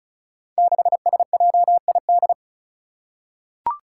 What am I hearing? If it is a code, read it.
Morse code, heard as a keyed tone, about 36 wpm, 700 Hz. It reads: 6SJID
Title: Five Challenging Letter-Number Combinations 35wpm